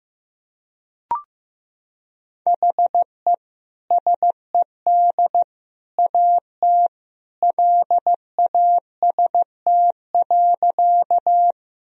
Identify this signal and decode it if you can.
Morse code, heard as a keyed tone, about 16 wpm, 700 Hz. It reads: HE SED AT LAST.
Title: Sentences from Top 300 Words and Abbreviations 15wpm